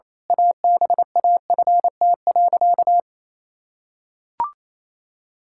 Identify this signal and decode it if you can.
Morse code, heard as a keyed tone, about 28 wpm, 700 Hz. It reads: ABAFT.
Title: Pangram Challenge v2 - 28wpm